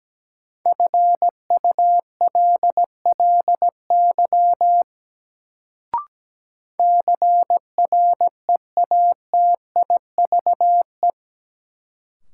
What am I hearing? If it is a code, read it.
Morse code, heard as a keyed tone, about 17 wpm, 700 Hz. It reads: FULLY CREATIVE
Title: Sentences from Top 1300 Words - New Word Drill Alt - 17wpm